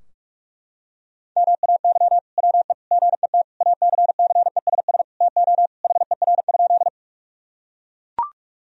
Morse code, heard as a keyed tone, about 41 wpm, 700 Hz. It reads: MAY WE GET ACCESS TO HER?
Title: Sentences from Top 1000 Words - Review Part 4 - 45wpm